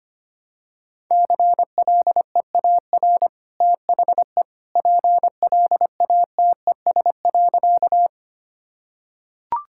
Morse code, heard as a keyed tone, about 25 wpm, 700 Hz. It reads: CLEAR THE PLATES.
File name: Easy Sentences with Hints - 25wpm